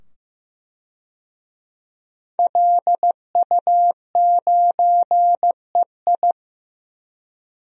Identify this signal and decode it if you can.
Morse code, heard as a keyed tone, about 16 wpm, 700 Hz. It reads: LU9EI
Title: DXCC-100 Call Signs - 5 Characters - 15wpm